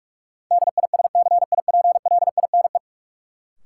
Morse code, heard as a keyed tone, about 47 wpm, 700 Hz. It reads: DISCIPLINE
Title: Sentences from Top 900 Words - New Word Drill - 45wpm